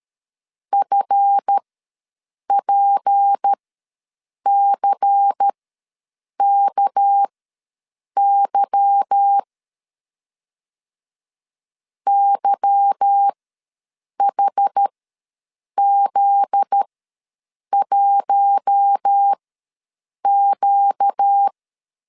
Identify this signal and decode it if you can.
Morse code, heard as a keyed tone, about 13 wpm, spaced as for 7 wpm, 800 Hz. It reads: FPCKY YHZ1Q
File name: LEZIONI / 9 2 ws 12wpm 3406